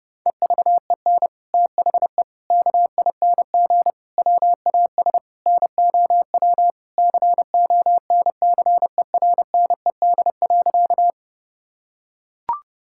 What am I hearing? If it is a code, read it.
Morse code, heard as a keyed tone, about 32 wpm, 700 Hz. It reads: EVEN THE KING WAS NOW CONCERNED.
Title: Sentences from Top 1000 Words - Review Part 6 - 30wpm